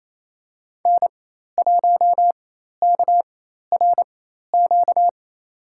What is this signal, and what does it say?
Morse code, heard as a keyed tone, about 27 wpm, spaced as for 13 wpm, 700 Hz. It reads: N1KRQ
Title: USA Call Signs - 4x Character Spacing - 28wpm